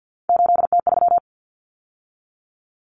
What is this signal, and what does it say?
Morse code, heard as a keyed tone, about 48 wpm, 700 Hz. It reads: 8T3
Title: Common Cut Numbers - 3-Digit 50wpm